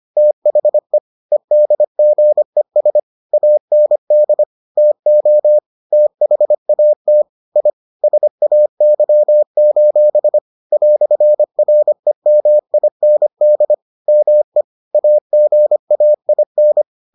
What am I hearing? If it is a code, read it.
Morse code, heard as a keyed tone, about 25 wpm, 600 Hz. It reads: THE EDGES AND TO THAT I SAY: "REMIND ME AGAIN